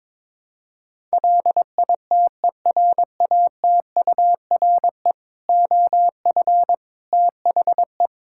Morse code, heard as a keyed tone, about 22 wpm, 700 Hz. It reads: LITERATURE OF THE
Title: Sentences from Top 1000 Words - Review Part 5 - 22wpm